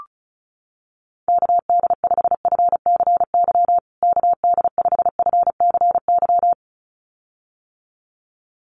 Morse code, heard as a keyed tone, about 34 wpm, 700 Hz. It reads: KD5FCY KD5FCY
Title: USA Call Signs Repeated 2x 35wpm